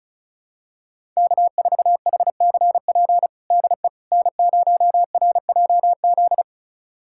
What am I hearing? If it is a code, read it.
Morse code, heard as a keyed tone, about 36 wpm, 700 Hz. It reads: K4HCP DE N0RJZ
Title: USA Callsign DE Callsign - 5 Characters - 35wpm